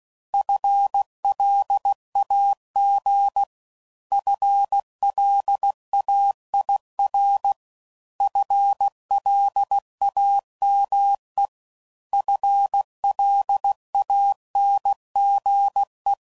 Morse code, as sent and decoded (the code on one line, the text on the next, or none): ..-. .-.. .- --. / ..-. .-.. .- .. .-. / ..-. .-.. .- -- . / ..-. .-.. .- -. --. .
FLAG FLAIR FLAME FLANGE